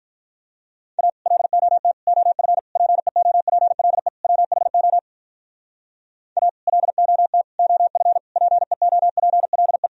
Morse code, heard as a keyed tone, about 45 wpm, 700 Hz. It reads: A LOT OF PEOPLE WHO A LOT OF PEOPLE